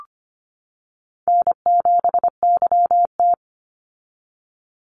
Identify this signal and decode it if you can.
Morse code, heard as a keyed tone, about 25 wpm, 700 Hz. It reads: N7YT
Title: Easy Call Signs - 25wpm